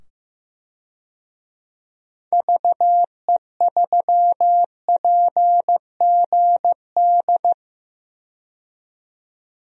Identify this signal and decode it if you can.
Morse code, heard as a keyed tone, about 15 wpm, 700 Hz. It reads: VE3PGD
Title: DXCC-100 Call Signs - 6 Characters - 15wpm